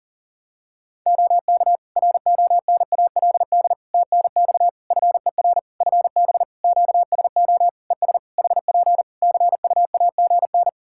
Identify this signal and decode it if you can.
Morse code, heard as a keyed tone, about 40 wpm, 700 Hz. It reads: OK RONALD TNX FER FB QSO ES HP CUAGN
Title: Practice QSO - Exchange 4 - Ending - 40wpm